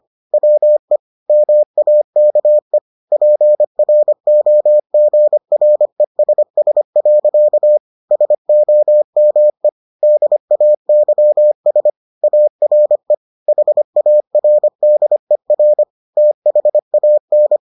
Morse code, heard as a keyed tone, about 25 wpm, 600 Hz. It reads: WE MAKE PROGRESS. SOME DAYS ARE HARDER THAN